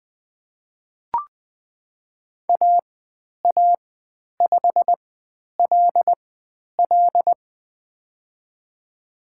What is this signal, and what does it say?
Morse code, heard as a keyed tone, about 20 wpm, spaced as for 10 wpm, 700 Hz. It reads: AA5LL
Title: 109 Lesson 18g - Letter m - Call Signs 20wpm